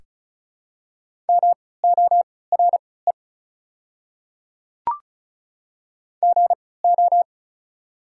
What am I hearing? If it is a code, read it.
Morse code, heard as a keyed tone, about 36 wpm, spaced as for 20 wpm, 700 Hz. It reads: MORE GO
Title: 3x Character Spacing - Top 100 Words - 35wpm